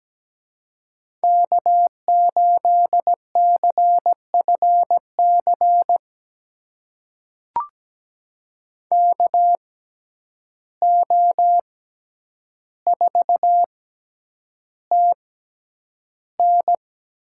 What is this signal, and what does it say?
Morse code, heard as a keyed tone, about 17 wpm, spaced as for 9 wpm, 700 Hz. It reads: K8CFC KO4TN
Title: USA Call Signs - 5 Characters - 6x Character Spacing - 17wpm